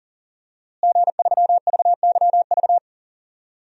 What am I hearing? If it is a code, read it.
Morse code, heard as a keyed tone, about 40 wpm, 700 Hz. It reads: G3VYV
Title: DXCC-100 Call Signs - 5 Characters - Repeated 3x - 40wpm